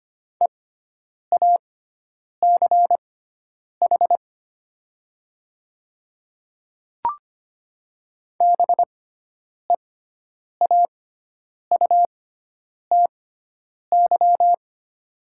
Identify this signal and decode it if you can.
Morse code, heard as a keyed tone, about 25 wpm, spaced as for 9 wpm, 700 Hz. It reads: EACH BEAUTY
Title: 6x Character Spacing - Top 500 Words - 25wpm